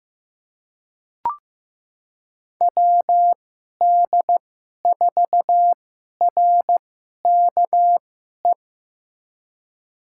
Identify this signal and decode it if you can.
Morse code, heard as a keyed tone, about 15 wpm, spaced as for 11 wpm, 700 Hz. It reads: WD4RKE
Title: USA Call Signs - 6 Characters - 2x Character Spacing - 15wpm